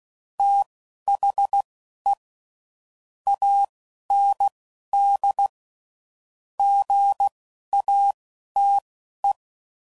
Morse code, heard as a keyed tone, about 16 wpm, spaced as for 11 wpm, 800 Hz. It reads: THE AND GATE
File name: CW Code Groups @10WPM / Technical Subjects / AND Gate